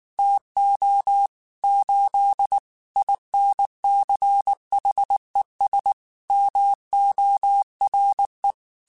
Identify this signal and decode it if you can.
Morse code, heard as a keyed tone, about 19 wpm, 800 Hz. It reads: TO 8 INCHES MORE
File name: QSOs / QSOs-mp3s / WA7QOX 1@18